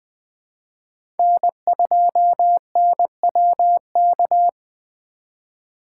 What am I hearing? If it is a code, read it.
Morse code, heard as a keyed tone, about 20 wpm, 700 Hz. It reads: N2NWK
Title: 2023 POTA Activator Call Signs - 20wpm